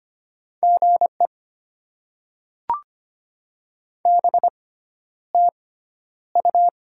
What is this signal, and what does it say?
Morse code, heard as a keyed tone, about 25 wpm, spaced as for 11 wpm, 700 Hz. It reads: GE BTU
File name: Top 20 QSO Elements - 6x Character Spacing - 25wpm